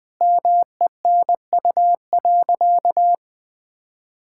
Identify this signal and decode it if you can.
Morse code, heard as a keyed tone, about 20 wpm, 700 Hz. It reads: MENU.
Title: Sentences with 4-Word Phrases - 20wpm